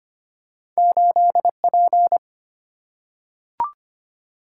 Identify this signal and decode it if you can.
Morse code, heard as a keyed tone, about 25 wpm, 700 Hz. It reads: 8P
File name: Two Letter-Number Combinations - 25wpm